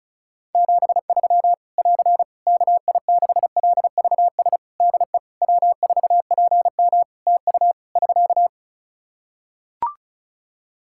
Morse code, heard as a keyed tone, about 34 wpm, 700 Hz. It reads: 73 <AR> KI6LVS DE W4PM TU <SK>
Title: Practice QSO - Exchange 4 - Ending - 35wpm